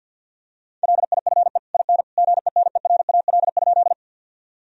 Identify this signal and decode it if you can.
Morse code, heard as a keyed tone, about 48 wpm, 700 Hz. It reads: LIFE IN GENERAL?
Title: Sentences from Top 1000 Words - Review Part 3 - 50wpm